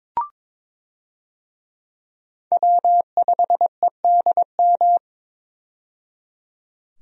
Morse code, heard as a keyed tone, about 22 wpm, 700 Hz. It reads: W5EDM
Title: USA Call Signs - 5 Characters - 22wpm